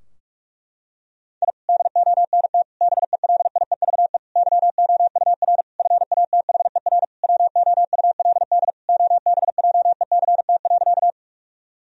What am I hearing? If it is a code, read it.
Morse code, heard as a keyed tone, about 43 wpm, 700 Hz. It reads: I DONT BELIEVE YOUR FATHER WOULD OBJECT.
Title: Sentences from Top 1000 Words - Review Part 3 - 45wpm